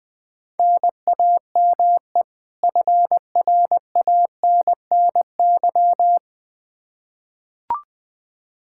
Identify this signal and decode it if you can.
Morse code, heard as a keyed tone, about 20 wpm, 700 Hz. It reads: NAME FRANNY
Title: Practice QSO - Exchange 1 - Beginning - 20wpm